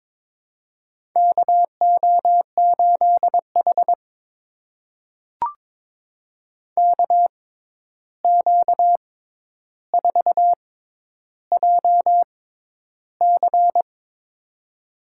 Morse code, heard as a keyed tone, about 22 wpm, spaced as for 11 wpm, 700 Hz. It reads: KO8H KQ4JC
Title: USA Call Signs - 6x Character Spacing - 22wpm